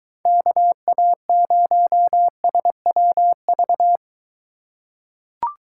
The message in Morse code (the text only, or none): KA0SWV